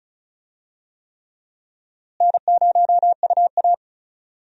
Morse code, heard as a keyed tone, about 36 wpm, 700 Hz. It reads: N0UA
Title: USA Call Signs - 4 Characters - Repeated 3x - 35wpm